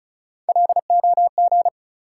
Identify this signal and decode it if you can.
Morse code, heard as a keyed tone, about 36 wpm, 700 Hz. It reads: LOG
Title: Sentences from Top 1400 Words - New Word Drill Alt - 35wpm